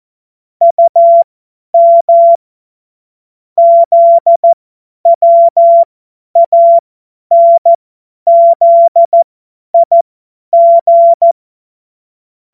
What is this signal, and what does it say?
Morse code, heard as a keyed tone, about 14 wpm, spaced as for 10 wpm, 680 Hz. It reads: UM ZWANZIG
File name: Morsekurs / Witze / Witz 21.November 2023 50 F70